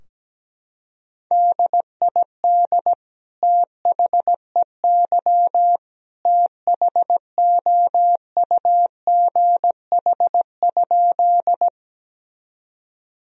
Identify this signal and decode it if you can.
Morse code, heard as a keyed tone, about 17 wpm, 700 Hz. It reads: DID THEY THOUGH?
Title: Sentences from Top 1000 Words - Review Part 2 - 17wpm